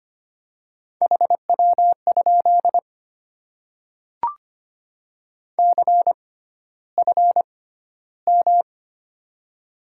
Morse code, heard as a keyed tone, about 25 wpm, spaced as for 13 wpm, 700 Hz. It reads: HW? CFM
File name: Top 90 QSO Elements - 6x Character Spacing - 25wpm